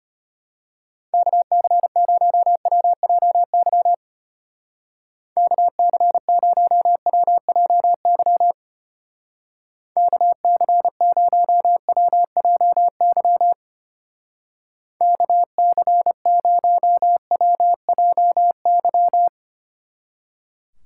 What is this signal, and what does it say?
Morse code, heard as a keyed tone, about 31 wpm, 700 Hz. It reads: KC0WJY KC0WJY KC0WJY KC0WJY
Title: Speed Racing - USA 2x3 Call Signs 25wpm